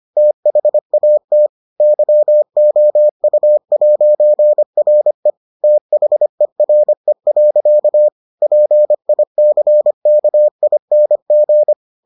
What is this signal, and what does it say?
Morse code, heard as a keyed tone, about 25 wpm, 600 Hz. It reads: THAT YOU'RE THERE. PICKING